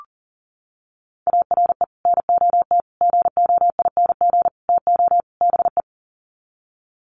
Morse code, heard as a keyed tone, about 40 wpm, 700 Hz. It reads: ARE NOT GOING TO BE
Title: Sets of 5 Words Encore 40wpm